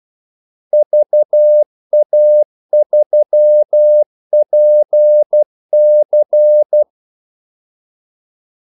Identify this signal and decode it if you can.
Morse code, heard as a keyed tone, about 12 wpm, 600 Hz. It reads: VA3PC
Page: 2 ekstra ordmellemrum mellem kaldesignalerne